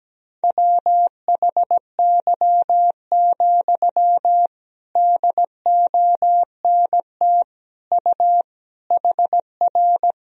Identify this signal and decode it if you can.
Morse code, heard as a keyed tone, about 17 wpm, 700 Hz. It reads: WHY, DONT U HR
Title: Sentences from Top 300 Words and Abbreviations 17wpm